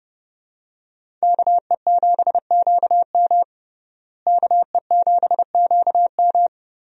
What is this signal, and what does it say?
Morse code, heard as a keyed tone, about 32 wpm, 700 Hz. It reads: KE7QM KE7QM
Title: USA Call Signs - 5 Characters - Repeated 2x - 30wpm